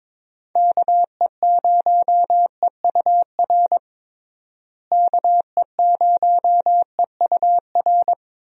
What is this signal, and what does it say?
Morse code, heard as a keyed tone, about 22 wpm, 700 Hz. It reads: KE0EUR KE0EUR
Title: USA Call Signs - 6 Characters - Repeated 2x - 22wpm